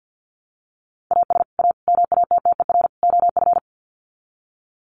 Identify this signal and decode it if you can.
Morse code, heard as a keyed tone, about 50 wpm, 700 Hz. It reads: AS A MATTER OF
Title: Sets of 4 Words 50wpm